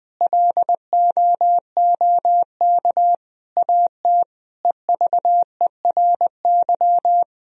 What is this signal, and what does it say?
Morse code, heard as a keyed tone, about 20 wpm, 700 Hz. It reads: LOOK AT EVERY
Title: Sentences from Top 1000 Words - Review Part 3 - 20wpm